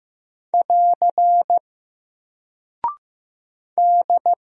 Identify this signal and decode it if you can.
Morse code, heard as a keyed tone, about 15 wpm, 700 Hz. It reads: <AR> D
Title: Single Character (Letter:Number:Punctuation:Prosign) Alt 15wpm